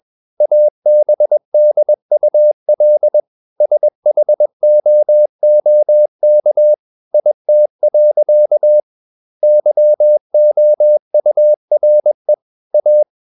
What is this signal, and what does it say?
Morse code, heard as a keyed tone, about 21 wpm, 600 Hz. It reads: ABDUL SHOOK IT. YOURE A
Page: CW med Gnister